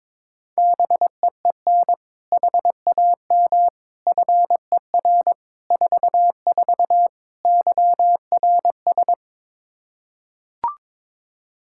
Morse code, heard as a keyed tone, about 22 wpm, 700 Hz. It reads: BEEN HAM FER 44 YRS
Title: Practice QSO - Exchange 3 - More Info - 22wpm